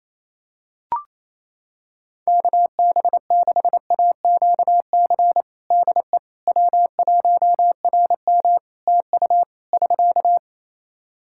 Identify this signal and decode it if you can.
Morse code, heard as a keyed tone, about 28 wpm, 700 Hz. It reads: KB6AQC DE W1RM TU <SK>
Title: Practice QSO - Exchange 4 - Ending - 28wpm